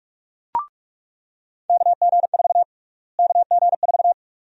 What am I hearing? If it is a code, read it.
Morse code, heard as a keyed tone, about 46 wpm, 700 Hz. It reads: KG4 KG4
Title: USA Call Sign Prefixes - Repeated 2x - 45wpm